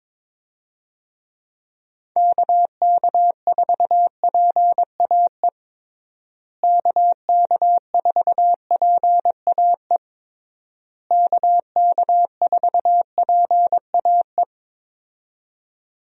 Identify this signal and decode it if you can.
Morse code, heard as a keyed tone, about 22 wpm, 700 Hz. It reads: KK4PAE KK4PAE KK4PAE
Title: USA Call Signs - 6 Characters - Repeated 3x - 22wpm